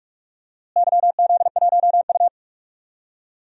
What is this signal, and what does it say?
Morse code, heard as a keyed tone, about 41 wpm, 700 Hz. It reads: YZ1U
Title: DXCC-100 Call Signs - 4 Characters - Repeated 3x - 45wpm